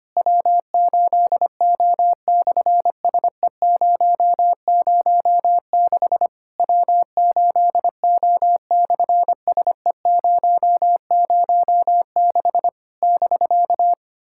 Morse code, heard as a keyed tone, about 25 wpm, 700 Hz. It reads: W8O/SE006 W8O/SE006 <BK>
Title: SOTA QSO Practice 25wpm